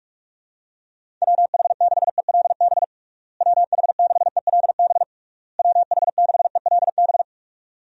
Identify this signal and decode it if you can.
Morse code, heard as a keyed tone, about 41 wpm, 700 Hz. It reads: WH6ELB WH6ELB WH6ELB
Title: USA Call Signs - 6 Characters - Repeated 3x - 45wpm